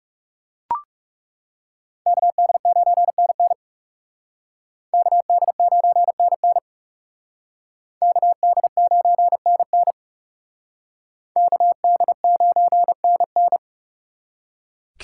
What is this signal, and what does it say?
Morse code, heard as a keyed tone, about 38 wpm, 700 Hz. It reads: KD9NN KD9NN KD9NN KD9NN
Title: Speed Racing - USA Call Signs 30wpm